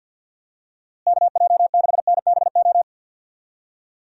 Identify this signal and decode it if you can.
Morse code, heard as a keyed tone, about 49 wpm, 700 Hz. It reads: KJ6NBO
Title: USA Call Signs - 6 Characters - Repeated 2x - 50wpm